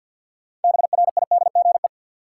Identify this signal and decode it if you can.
Morse code, heard as a keyed tone, about 53 wpm, 700 Hz. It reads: BRIDGE
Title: Sentences from Top 1200 Words - New Word Drill - 50wpm